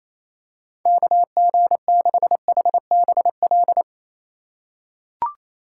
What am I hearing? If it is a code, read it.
Morse code, heard as a keyed tone, about 27 wpm, 700 Hz. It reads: KG6HBL
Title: USA Call Signs - 6 Characters - 3x Character Spacing - 28wpm